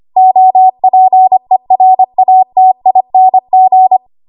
Morse code, heard as a keyed tone, about 25 wpm, 750 Hz. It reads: OPERATING
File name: Archive / 25 WPM / 25WPM